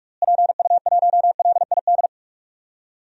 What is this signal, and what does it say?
Morse code, heard as a keyed tone, about 46 wpm, 700 Hz. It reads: PU1LID
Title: DXCC-100 Call Signs - 6 Characters - Repeated 3x - 45wpm